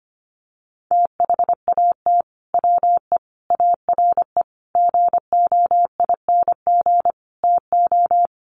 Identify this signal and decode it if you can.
Morse code, heard as a keyed tone, about 25 wpm, 700 Hz. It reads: THAT WE ARE GOING TO
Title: Sets of 5 Words Encore 25wpm